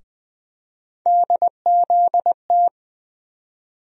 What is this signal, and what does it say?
Morse code, heard as a keyed tone, about 20 wpm, 700 Hz. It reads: DZT
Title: USA Call Sign Postfixes - 3 Letters 20wpm